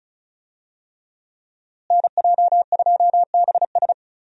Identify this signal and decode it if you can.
Morse code, heard as a keyed tone, about 36 wpm, 700 Hz. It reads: NJ2BS
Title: USA Call Signs - 5 Characters - Repeated 3x - 35wpm